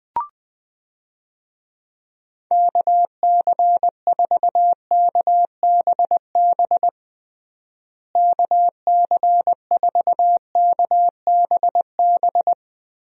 Morse code, heard as a keyed tone, about 20 wpm, 700 Hz. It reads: KC4KBB KC4KBB
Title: USA Call Signs - 6 Characters - Repeated 3x - 20wpm